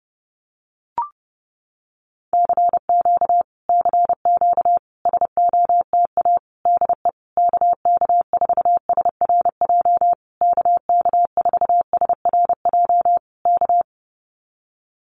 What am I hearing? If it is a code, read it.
Morse code, heard as a keyed tone, about 30 wpm, 700 Hz. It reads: CQ CQ SOTA DE KK4SRJ KK4SRJ K
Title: Common QSO Phrases Encore 30wpm